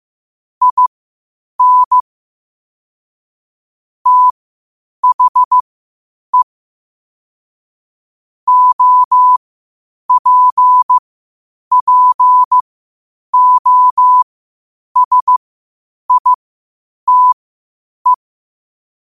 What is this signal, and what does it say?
Morse code, heard as a keyed tone, about 15 wpm, spaced as for 8 wpm, 1000 Hz. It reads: IN THE OPPOSITE
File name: qotdfiles / 8wpm